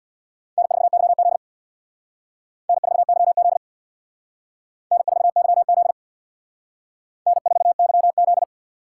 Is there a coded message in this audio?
Yes: Morse code from 0.6 to 8.4 s, heard as a keyed tone, about 63 wpm, 700 Hz. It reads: N4YZ N4YZ N4YZ N4YZ